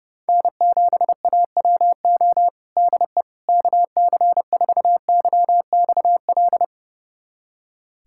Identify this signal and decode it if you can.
Morse code, heard as a keyed tone, about 30 wpm, 700 Hz. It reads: N7AWO DE KC4YXL
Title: USA Callsign DE Callsign 30wpm